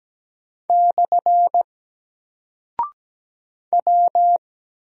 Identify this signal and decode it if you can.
Morse code, heard as a keyed tone, about 17 wpm, 700 Hz. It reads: / W
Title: Single Character (Letter:Number:Punctuation:Prosign) Alt 17wpm